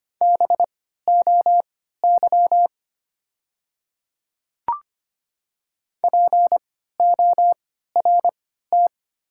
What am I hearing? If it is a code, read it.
Morse code, heard as a keyed tone, about 25 wpm, spaced as for 14 wpm, 690 Hz. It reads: BOY PORT
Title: 3x Character Spacing - Top 200 Words - 25wpm